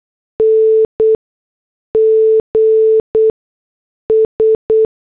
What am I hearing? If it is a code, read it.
Morse code, heard as a keyed tone, about 8 wpm, spaced as for 6 wpm, 440 Hz. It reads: NGS
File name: morse / headlines